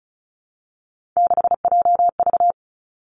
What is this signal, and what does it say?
Morse code, heard as a keyed tone, about 36 wpm, 700 Hz. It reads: 6JV
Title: Three Challenging Letter-Number Combinations 35wpm